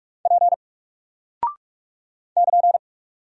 Morse code, heard as a keyed tone, about 45 wpm, 700 Hz. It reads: P <KN>
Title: Single Character (Letter:Number:Punctuation:Prosign) Alt 45wpm